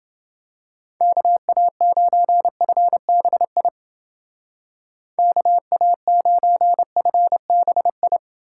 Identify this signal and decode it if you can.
Morse code, heard as a keyed tone, about 29 wpm, 700 Hz. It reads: KA9FBI KA9FBI